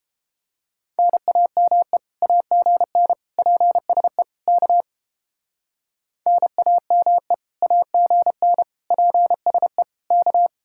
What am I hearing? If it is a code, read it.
Morse code, heard as a keyed tone, about 29 wpm, 700 Hz. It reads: NAME AGN PSE K NAME AGN PSE K